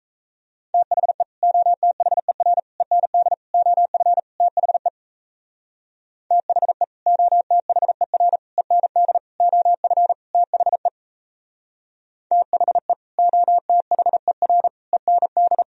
Morse code, heard as a keyed tone, about 35 wpm, 700 Hz. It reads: THE OTHER END OF THE THE OTHER END OF THE THE OTHER END